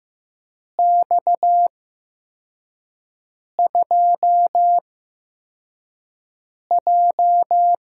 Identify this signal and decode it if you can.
Morse code, heard as a keyed tone, about 15 wpm, spaced as for 4 wpm, 700 Hz. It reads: X2J
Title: USA Call Signs - 4 Characters - 8x Character Spacing - 15wpm